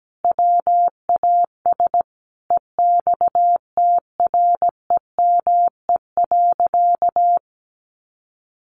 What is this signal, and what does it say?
Morse code, heard as a keyed tone, about 17 wpm, 700 Hz. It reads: WAS EXTREME.
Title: Sentences from Top 1500 Words 17wpm